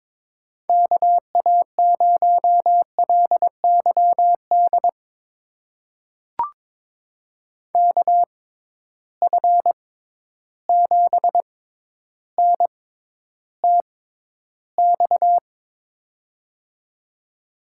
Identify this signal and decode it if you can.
Morse code, heard as a keyed tone, about 22 wpm, spaced as for 11 wpm, 700 Hz. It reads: KA0LYD KF7NTX
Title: USA Call Signs - 6 Characters - 6x Character Spacing - 22wpm